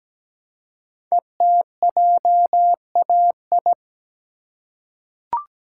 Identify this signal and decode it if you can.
Morse code, heard as a keyed tone, about 17 wpm, 700 Hz. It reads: ETJAI
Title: Code Groups - 5 Letters - 17wpm